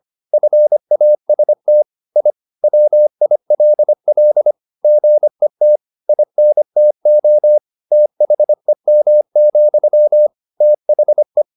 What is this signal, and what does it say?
Morse code, heard as a keyed tone, about 25 wpm, 600 Hz. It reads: FAST I WILL GET INTO THEM, THE
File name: the old man and the sea part 1 25wpm